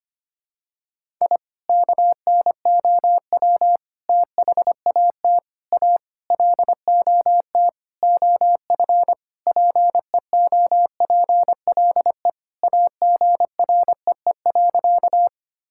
Morse code, heard as a keyed tone, about 25 wpm, 700 Hz. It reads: I KNOW THAT A LOT OF PEOPLE AGREE.
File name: practice / 25wpm / Sentences with 4-Word Phrases - 25wpm